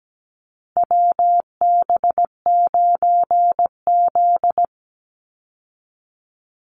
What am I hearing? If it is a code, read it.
Morse code, heard as a keyed tone, about 17 wpm, 700 Hz. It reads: WB9Z
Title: Call Signs Repeated 2x - 17wpm